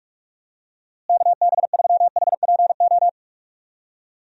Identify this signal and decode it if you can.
Morse code, heard as a keyed tone, about 43 wpm, 700 Hz. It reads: KB3HPO
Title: USA Call Signs - 6 Characters - Repeated 3x - 45wpm